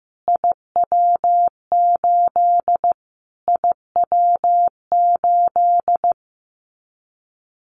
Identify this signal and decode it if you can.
Morse code, heard as a keyed tone, about 15 wpm, 700 Hz. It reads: IW8 IW8
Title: Callsign Prefixes for Top 25 DXCC - Repeated 3x - 15wpm